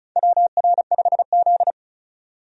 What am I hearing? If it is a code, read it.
Morse code, heard as a keyed tone, about 36 wpm, 700 Hz. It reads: WR5Z
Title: USA Call Signs - 4 Characters - 35wpm